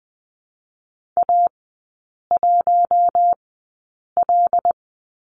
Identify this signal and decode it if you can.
Morse code, heard as a keyed tone, about 20 wpm, spaced as for 8 wpm, 700 Hz. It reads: A1L
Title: 074 Lesson 13e - Letter L - Three Characters 20wpm